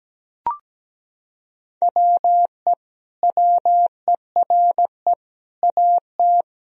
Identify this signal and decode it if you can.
Morse code, heard as a keyed tone, about 17 wpm, 700 Hz. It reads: WE WERE AT
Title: Sentences from Top 1000 Words - Review Part 1 - 17wpm